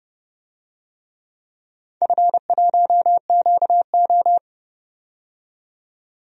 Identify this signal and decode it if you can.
Morse code, heard as a keyed tone, about 30 wpm, 700 Hz. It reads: F1QO
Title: DXCC-100 Call Signs - 4 Characters - 30wpm